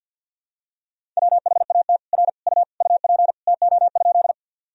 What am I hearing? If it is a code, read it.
Morse code, heard as a keyed tone, about 51 wpm, 700 Hz. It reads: WHAT R U UP TO?
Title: Sentences from Top 100 Words and Abbreviations 50wpm